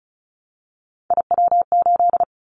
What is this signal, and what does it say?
Morse code, heard as a keyed tone, about 36 wpm, 700 Hz. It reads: IW8
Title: Callsign Prefixes for Top 50 DXCC - 35wpm